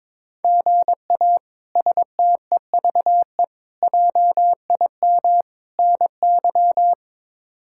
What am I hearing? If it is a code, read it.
Morse code, heard as a keyed tone, about 22 wpm, 700 Hz. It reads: GA STEVE JIM NY
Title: K1USN SST QSO Practice - 22wpm